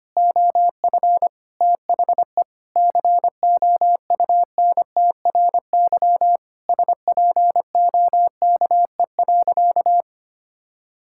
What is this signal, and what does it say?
Morse code, heard as a keyed tone, about 25 wpm, 700 Hz. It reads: OF THE COUNTRY SPOKE.
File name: practice / 25wpm / Sentences with 3-Word Phrases - 25wpm